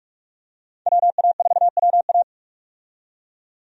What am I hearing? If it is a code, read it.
Morse code, heard as a keyed tone, about 41 wpm, 700 Hz. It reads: WA4WA
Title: USA Call Signs - 5 Characters - 45wpm